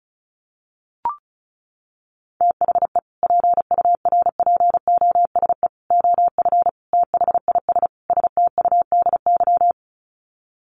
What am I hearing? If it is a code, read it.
Morse code, heard as a keyed tone, about 34 wpm, 700 Hz. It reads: THE PURPOSE OF THIS STUDY
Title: Sets of 5 Words 35wpm